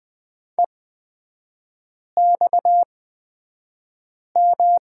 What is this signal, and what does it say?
Morse code, heard as a keyed tone, about 20 wpm, spaced as for 5 wpm, 700 Hz. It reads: EXM
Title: 229 Lesson 37e - Letter x - Three Characters 20wpm